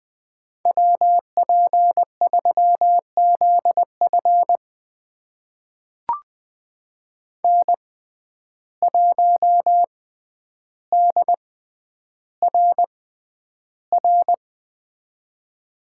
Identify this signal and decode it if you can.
Morse code, heard as a keyed tone, about 20 wpm, spaced as for 10 wpm, 700 Hz. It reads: WP3ZF N1DRR
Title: USA Call Signs - 5 Characters - 6x Character Spacing - 20wpm